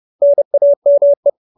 Morse code, heard as a keyed tone, about 30 wpm, 570 Hz. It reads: NAME